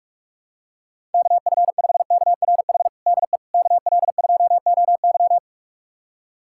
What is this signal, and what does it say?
Morse code, heard as a keyed tone, about 41 wpm, 700 Hz. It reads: KF5KRH DE KL2OY
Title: USA Callsign DE Callsign 45wpm